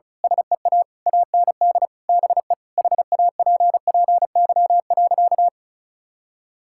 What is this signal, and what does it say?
Morse code, heard as a keyed tone, about 36 wpm, 700 Hz. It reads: SEA AND BE HAPPY.
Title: Sentences from Top 1000 Words - Review Part 5 - 35wpm